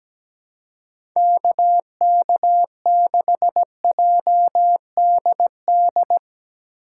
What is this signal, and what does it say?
Morse code, heard as a keyed tone, about 17 wpm, 700 Hz. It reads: KK6JDD
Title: USA Call Signs - 3x Character Spacing - 17wpm